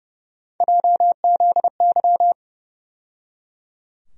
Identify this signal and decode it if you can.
Morse code, heard as a keyed tone, about 30 wpm, 700 Hz. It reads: JZY
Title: USA Call Sign Postfixes - 3 Letters 30wpm